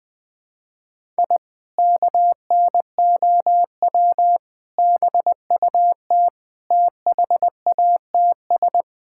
Morse code, heard as a keyed tone, about 20 wpm, 700 Hz. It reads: I KNOW BUT THATS
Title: Sentences with 2-Word Phrases - 20wpm